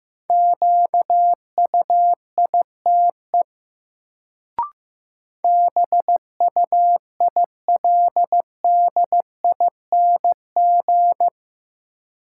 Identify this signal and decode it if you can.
Morse code, heard as a keyed tone, about 15 wpm, 700 Hz. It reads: QUITE BUILDING
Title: Sentences from Top 700 Words - New Word Drill - 15wpm